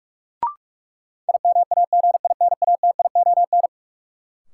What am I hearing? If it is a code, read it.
Morse code, heard as a keyed tone, about 46 wpm, 700 Hz. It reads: IMAGINATION
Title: Sentences from Top 1300 Words - New Word Drill Alt - 45wpm